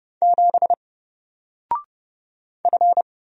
Morse code, heard as a keyed tone, about 30 wpm, 700 Hz. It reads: 7 F
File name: Single-Letter-Number-30wpm